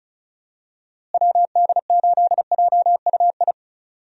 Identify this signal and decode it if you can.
Morse code, heard as a keyed tone, about 36 wpm, 700 Hz. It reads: WD8JUI
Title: USA Call Signs - 6 Characters - Repeated 3x - 35wpm